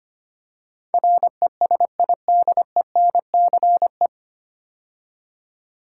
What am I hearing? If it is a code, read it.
Morse code, heard as a keyed tone, about 25 wpm, 700 Hz. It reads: RESIDENCE
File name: practice / 25wpm / 9 Letter Words - First 3 Letters Given - 25wpm